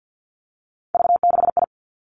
Morse code, heard as a keyed tone, about 50 wpm, 700 Hz. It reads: V6I
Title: Three Challenging Letter-Number Combinations 50wpm